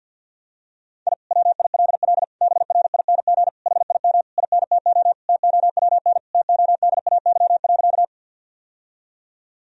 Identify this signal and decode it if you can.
Morse code, heard as a keyed tone, about 50 wpm, 700 Hz. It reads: I WILL BRING HIM INTO TOWN TODAY.
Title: Sentences from Top 1000 Words - Review Part 3 - 50wpm